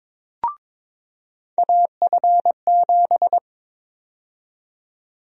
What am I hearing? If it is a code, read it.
Morse code, heard as a keyed tone, about 22 wpm, 700 Hz. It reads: AF7
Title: USA Call Sign Prefixes - 22wpm